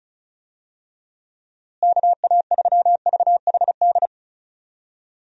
Morse code, heard as a keyed tone, about 36 wpm, 700 Hz. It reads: KA3VHD
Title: USA Call Signs - 6 Characters - Repeated 2x - 35wpm